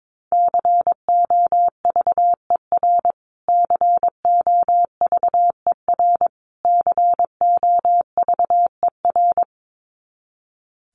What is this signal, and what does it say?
Morse code, heard as a keyed tone, about 22 wpm, 700 Hz. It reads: COVER COVER COVER
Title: Top 300 Words Repeated 22wpm